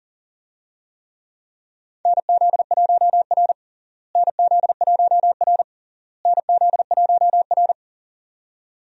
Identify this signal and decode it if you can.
Morse code, heard as a keyed tone, about 40 wpm, 700 Hz. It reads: NZ1R NZ1R NZ1R
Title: USA Call Signs - 4 Characters - Repeated 3x - 40wpm